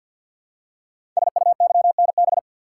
Morse code, heard as a keyed tone, about 50 wpm, 700 Hz. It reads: SUYNB
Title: Code Groups - 5 Letters - 50wpm